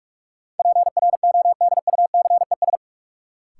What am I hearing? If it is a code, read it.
Morse code, heard as a keyed tone, about 46 wpm, 700 Hz. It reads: PRODUCES
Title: Sentences from Top 1000 Words - Review Part 6 - New Word Drill - 45wpm